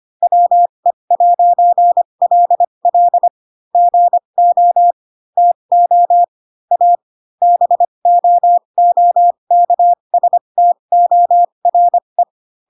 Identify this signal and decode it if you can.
Morse code, heard as a keyed tone, about 25 wpm, spaced as for 22 wpm, 700 Hz. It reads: WE'LL GO TO A BOOKSTORE